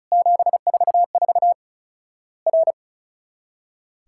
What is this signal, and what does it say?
Morse code, heard as a keyed tone, about 34 wpm, 700 Hz. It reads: 744 R
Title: Worked All Europe DX Contest - QTC Exchange - 35wpm